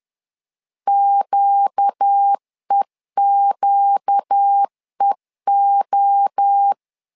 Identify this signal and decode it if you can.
Morse code, heard as a keyed tone, about 10 wpm, 800 Hz. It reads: QEQEO
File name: LEZIONI / 4 Q 10wpm 2210